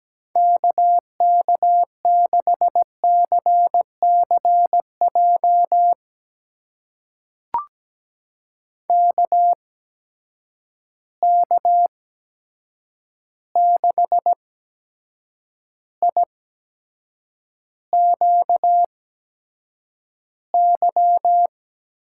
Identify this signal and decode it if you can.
Morse code, heard as a keyed tone, about 17 wpm, spaced as for 7 wpm, 700 Hz. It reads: KK6CCJ KK6IQY
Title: USA Call Signs - 6 Characters - 8x Character Spacing - 17wpm